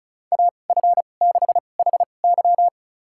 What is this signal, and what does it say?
Morse code, heard as a keyed tone, about 36 wpm, spaced as for 25 wpm, 700 Hz. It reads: AF6HY
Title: USA Call Signs - 5 Characters - 2x Character Spacing - 35wpm